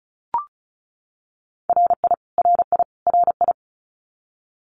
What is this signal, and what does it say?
Morse code, heard as a keyed tone, about 34 wpm, 700 Hz. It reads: RI RI RI
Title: US States Repeated 3x 35wpm